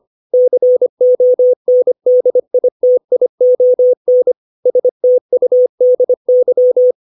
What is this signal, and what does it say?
Morse code, heard as a keyed tone, about 25 wpm, 500 Hz. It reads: CONDITION STUDY
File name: 632 words / 632 Words 25WPM